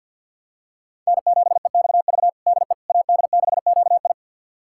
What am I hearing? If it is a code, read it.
Morse code, heard as a keyed tone, about 51 wpm, 700 Hz. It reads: N7EXV DE AD6QI
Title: USA Callsign DE Callsign - 5 Characters - 50wpm